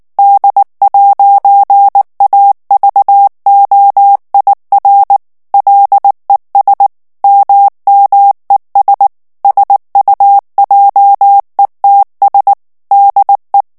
Morse code, heard as a keyed tone, about 19 wpm, 800 Hz. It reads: D'AVOIR LES MMES SUJETS DE
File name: FRANCAIS TEXTE CLAIR 2 VITESSE 18 MOTS MINUTES